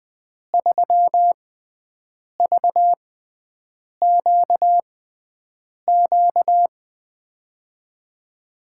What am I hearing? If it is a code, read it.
Morse code, heard as a keyed tone, about 21 wpm, spaced as for 7 wpm, 700 Hz. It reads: 3VQQ
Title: USA Call Signs - 6x Character Spacing - 20wpm